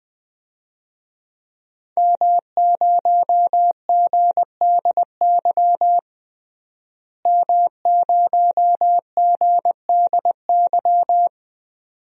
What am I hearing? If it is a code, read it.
Morse code, heard as a keyed tone, about 20 wpm, 700 Hz. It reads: M0GDY M0GDY
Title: DXCC-100 Call Signs - 5 Characters - Repeated 2x - 20wpm